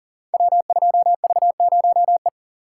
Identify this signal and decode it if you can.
Morse code, heard as a keyed tone, about 42 wpm, 700 Hz. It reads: W2V0E
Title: Code Groups - 5 Letter:Numbers - 40wpm